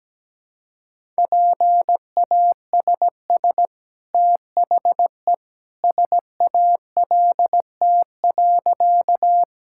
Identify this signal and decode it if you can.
Morse code, heard as a keyed tone, about 17 wpm, 700 Hz. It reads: PASS THE SALT.
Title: Sentences from Top 1000 Words - Review Part 5 - 17wpm